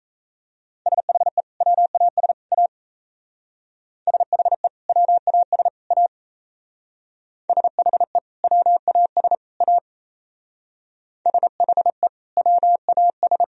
Morse code, heard as a keyed tone, about 37 wpm, 700 Hz. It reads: SHE WAS A SHE WAS A SHE WAS A SHE WAS